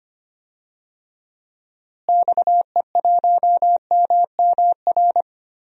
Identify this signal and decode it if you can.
Morse code, heard as a keyed tone, about 25 wpm, 700 Hz. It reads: XE1MMR
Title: DXCC-100 Call Signs - 6 Characters - Repeated 3x - 25wpm